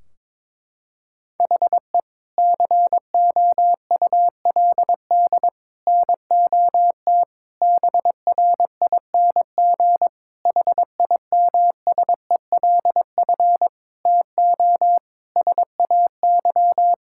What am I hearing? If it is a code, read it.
Morse code, heard as a keyed tone, about 22 wpm, 700 Hz. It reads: HE COULD NOT BRING HIMSELF TO SAY
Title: Sentences from Top 1600 Words 22wpm